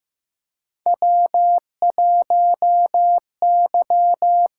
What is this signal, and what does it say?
Morse code, heard as a keyed tone, about 15 wpm, 700 Hz. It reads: W 1 Y
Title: USA Call Signs - 5 Characters - 6x Character Spacing - 15wpm